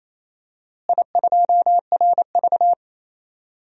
Morse code, heard as a keyed tone, about 27 wpm, 700 Hz. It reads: I2RV
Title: DXCC-100 Call Signs - 4 Characters - Repeated 2x - 28wpm